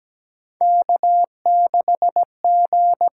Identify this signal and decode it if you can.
Morse code, heard as a keyed tone, about 17 wpm, 700 Hz. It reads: K6G
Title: USA Call Signs - 5 Characters - 6x Character Spacing - 17wpm